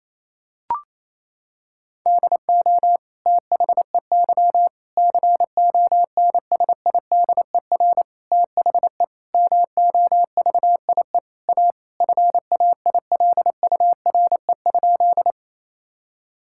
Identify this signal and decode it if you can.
Morse code, heard as a keyed tone, about 28 wpm, 700 Hz. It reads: DO THEY CONSIDER THE MOVIE A FAILURE?
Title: Sentences from Top 1000 Words - Review Part 6 - 28wpm